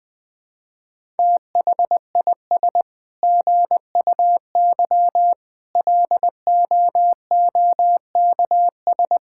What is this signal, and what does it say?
Morse code, heard as a keyed tone, about 21 wpm, 700 Hz. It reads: THIS GUY LOOKS
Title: Sets of 9 Words - 20wpm